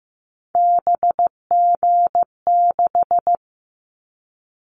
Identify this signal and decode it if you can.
Morse code, heard as a keyed tone, about 15 wpm, 700 Hz. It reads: BG6
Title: Callsign Prefixes for Top 100 DXCC - 15wpm